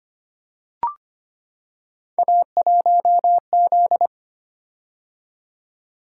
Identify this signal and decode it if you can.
Morse code, heard as a keyed tone, about 25 wpm, 700 Hz. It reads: A1Z
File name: Three Letter-Number Combinations - 25wpm